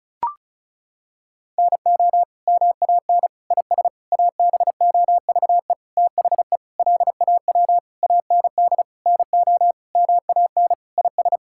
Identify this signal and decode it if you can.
Morse code, heard as a keyed tone, about 36 wpm, 700 Hz. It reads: NO MAN IS ABOVE THE LAW AND NO MAN IS
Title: Sentences from Top 1000 Words - Review Part 5 - 35wpm